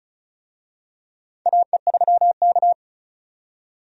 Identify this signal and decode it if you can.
Morse code, heard as a keyed tone, about 36 wpm, 700 Hz. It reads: AE3K
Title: USA Call Signs - 4 Characters - 35wpm